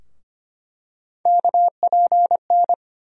Morse code, heard as a keyed tone, about 25 wpm, 700 Hz. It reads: KPN
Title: USA Call Sign Postfixes - 3 Letters 25wpm